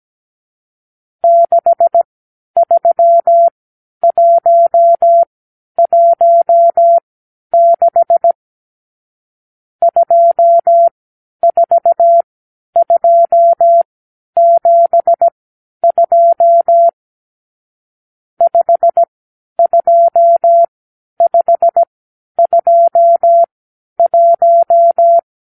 Morse code, heard as a keyed tone, about 17 wpm, spaced as for 10 wpm, 680 Hz. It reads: 63116 24272 52521
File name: k7qo / Lesson 52 (8)